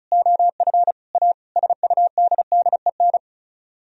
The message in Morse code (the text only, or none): OF A SUDDEN